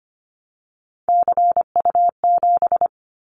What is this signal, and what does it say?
Morse code, heard as a keyed tone, about 25 wpm, 700 Hz. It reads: CU7
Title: Callsign Prefixes for Top 100 DXCC - 25wpm